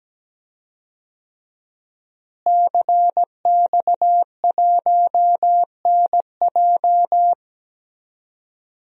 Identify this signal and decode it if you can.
Morse code, heard as a keyed tone, about 17 wpm, 700 Hz. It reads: CX1NJ
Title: DXCC-100 Call Signs - 5 Characters - Repeated 2x - 17wpm